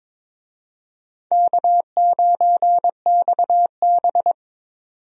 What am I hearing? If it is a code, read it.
Morse code, heard as a keyed tone, about 22 wpm, 700 Hz. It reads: K9XB
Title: USA Call Signs - 4 Characters - 22wpm